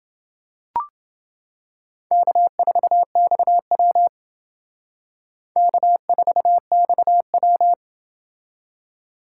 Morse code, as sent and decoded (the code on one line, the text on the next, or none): -.- ....- -..- .-- / -.- ....- -..- .--
K4XW K4XW